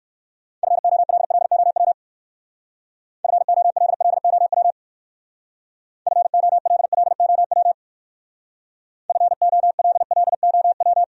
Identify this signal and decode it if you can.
Morse code, heard as a keyed tone, about 56 wpm, 700 Hz. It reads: FOLLOW FOLLOW FOLLOW FOLLOW